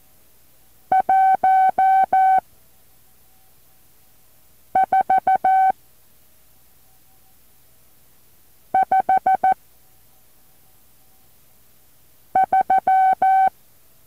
Morse code, heard as a keyed tone, about 14 wpm, 760 Hz. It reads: 1 4 5 3
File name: intl morse code2